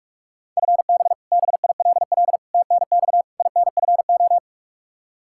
Morse code, heard as a keyed tone, about 44 wpm, 700 Hz. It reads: FB BILL TNX INFO
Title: Practice QSO - Exchange 3 - More Info - 45wpm